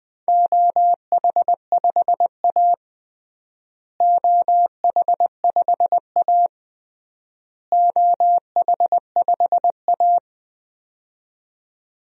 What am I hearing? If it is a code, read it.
Morse code, heard as a keyed tone, about 20 wpm, 700 Hz. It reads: OH5A OH5A OH5A
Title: DXCC-100 Call Signs - 4 Characters - Repeated 3x - 20wpm